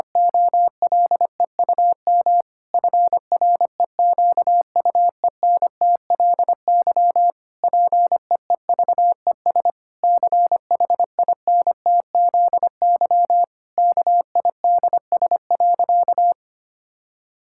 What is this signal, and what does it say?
Morse code, heard as a keyed tone, about 25 wpm, 700 Hz. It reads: OLEUM FREQUENTLY PEEVES CHINTZY KIDS.
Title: Pangram Challenge v2 - 25wpm